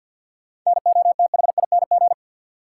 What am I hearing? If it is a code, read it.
Morse code, heard as a keyed tone, about 53 wpm, 700 Hz. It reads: NOTHING
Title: Sets of 4 Words - New Word Drill - 50wpm